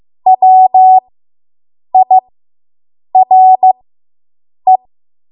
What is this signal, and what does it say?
Morse code, heard as a keyed tone, about 15 wpm, spaced as for 7 wpm, 750 Hz. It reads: WIRE